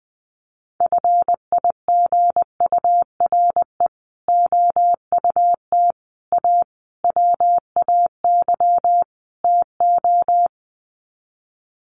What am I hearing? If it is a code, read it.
Morse code, heard as a keyed tone, about 20 wpm, 700 Hz. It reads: FIGURE OUT A WAY TO
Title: Sets of 5 Words Encore2 20wpm